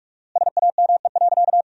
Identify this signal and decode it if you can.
Morse code, heard as a keyed tone, about 47 wpm, 700 Hz. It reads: SAME.
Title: 4x Word-Spacing - Sentences from Top 300 Words - 45wpm